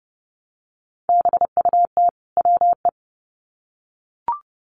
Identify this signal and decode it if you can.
Morse code, heard as a keyed tone, about 30 wpm, 700 Hz. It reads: BUT WE
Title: Sets of 2 Words 30wpm